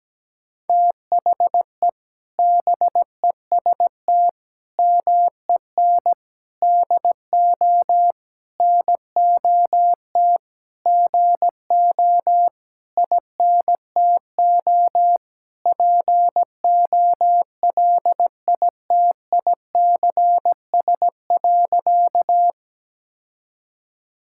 Morse code, heard as a keyed tone, about 17 wpm, 700 Hz. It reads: THE BEST MEN DO NOT GO INTO POLITICS.
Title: Sentences from Top 1000 Words - Review Part 6 - 17wpm